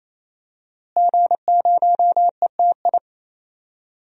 Morse code, heard as a keyed tone, about 28 wpm, 700 Hz. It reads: G0ETI
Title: DXCC-100 Call Signs - 5 Characters - Repeated 3x - 28wpm